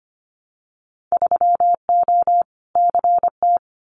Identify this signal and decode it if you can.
Morse code, heard as a keyed tone, about 25 wpm, 700 Hz. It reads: 3O CT
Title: Winter Field Day Exchanges Encore 25wpm